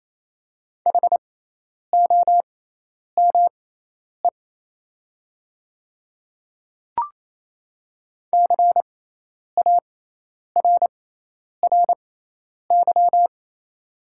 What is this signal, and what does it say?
Morse code, heard as a keyed tone, about 28 wpm, spaced as for 10 wpm, 700 Hz. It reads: HOME CARRY
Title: 6x Character Spacing - Top 500 Words - 28wpm